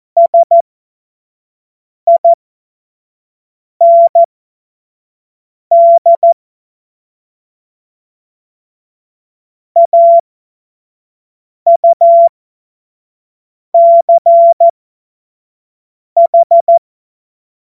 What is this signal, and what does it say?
Morse code, heard as a keyed tone, about 14 wpm, spaced as for 5 wpm, 680 Hz. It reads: SIND AUCH
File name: Morsekurs / Witze / Witz 19 März 2024 25 F70